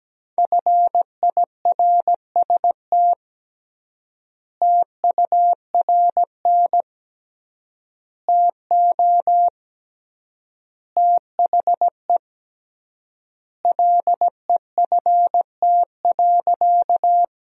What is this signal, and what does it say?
Morse code, heard as a keyed tone, about 17 wpm, 700 Hz. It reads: FIRST TURN TO THE LEFT.
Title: 3x Word-Spacing - Sentences from Top 300 Words - 17wpm